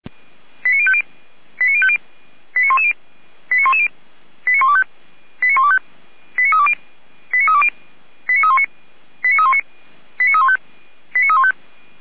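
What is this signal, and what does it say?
Five-tone selective call (ZVEI), heard as quick runs of five tones, tones of about 70 ms. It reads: 89050 89050 89110 89110 89125 89125 89320 89320 89319 89319 89315 89315
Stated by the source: ZVEI five-tone selective call received on SDR, 455.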